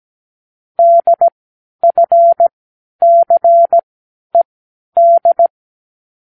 Morse code, heard as a keyed tone, about 17 wpm, spaced as for 10 wpm, 680 Hz. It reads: DFCED
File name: k7qo / Lesson9(A-F)